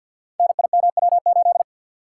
Morse code, heard as a keyed tone, about 45 wpm, 700 Hz. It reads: NIMW8
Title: Code Groups - 5 Letter:Numbers - 50wpm